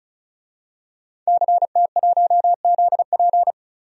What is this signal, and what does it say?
Morse code, heard as a keyed tone, about 36 wpm, 700 Hz. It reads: CT1ZP
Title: DXCC-100 Call Signs - 5 Characters - 35wpm